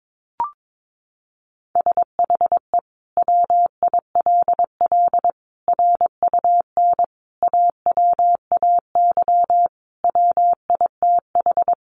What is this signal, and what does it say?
Morse code, heard as a keyed tone, about 22 wpm, 700 Hz. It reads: SHE WILL RUN AWAY WITH
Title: Sentences from Top 600 Words 22wpm